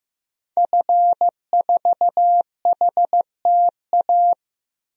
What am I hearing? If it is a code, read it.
Morse code, heard as a keyed tone, about 15 wpm, 700 Hz. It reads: F4HTA
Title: DXCC-100 Call Signs - 5 Characters - 15wpm